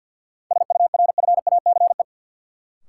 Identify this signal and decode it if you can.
Morse code, heard as a keyed tone, about 49 wpm, 700 Hz. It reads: SURFACE
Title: Sets of 4 Words - New Word Drill - 50wpm